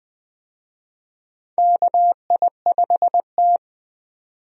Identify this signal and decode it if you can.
Morse code, heard as a keyed tone, about 20 wpm, 700 Hz. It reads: KI5T
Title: USA Call Signs - 4 Characters - 20wpm